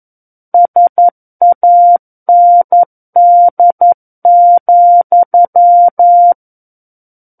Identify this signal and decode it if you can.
Morse code, heard as a keyed tone, about 11 wpm, 700 Hz. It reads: SAND,
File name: 11 WPM